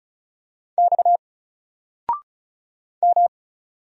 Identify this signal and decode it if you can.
Morse code, heard as a keyed tone, about 36 wpm, 700 Hz. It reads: X M
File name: Single-Letter-Number-35wpm